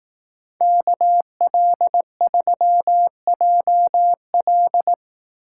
Morse code, heard as a keyed tone, about 18 wpm, 700 Hz. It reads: KL3JL
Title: Speed Racing - USA 2x2 Call Signs 15wpm